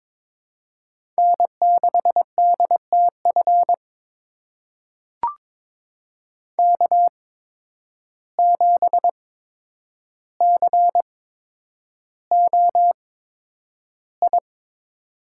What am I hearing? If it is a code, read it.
Morse code, heard as a keyed tone, about 22 wpm, spaced as for 9 wpm, 700 Hz. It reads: N6DTF K7COI
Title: USA Call Signs - 5 Characters - 8x Character Spacing - 22wpm